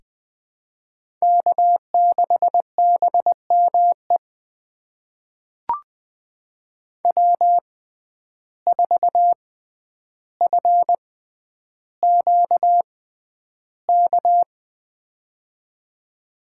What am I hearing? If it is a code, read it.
Morse code, heard as a keyed tone, about 20 wpm, spaced as for 10 wpm, 700 Hz. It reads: K6BME W4FQK
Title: USA Call Signs - 5 Characters - 6x Character Spacing - 20wpm